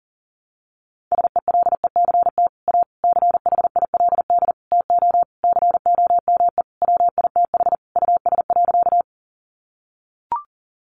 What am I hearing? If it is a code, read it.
Morse code, heard as a keyed tone, about 40 wpm, 700 Hz. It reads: SELECT A CHILD TO COME WITH US.
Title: Sentences from Top 1300 Words 40wpm